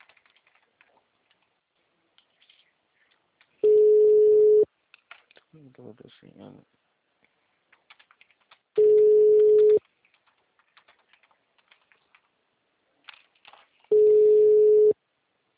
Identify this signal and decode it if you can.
Ringback tone (Europe, 425 Hz), with speech.